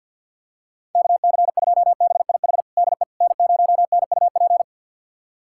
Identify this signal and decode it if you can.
Morse code, heard as a keyed tone, about 50 wpm, 700 Hz. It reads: KC2BIH DE N0NUP
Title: USA Callsign DE Callsign 50wpm